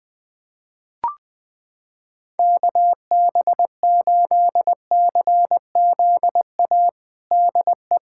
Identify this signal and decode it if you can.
Morse code, heard as a keyed tone, about 21 wpm, 700 Hz. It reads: KB8CZA DE
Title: USA Callsign DE Callsign 20wpm